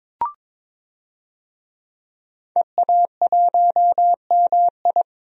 Morse code, heard as a keyed tone, about 21 wpm, 700 Hz. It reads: EA1MI
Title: DXCC-100 Call Signs - 5 Characters - 22wpm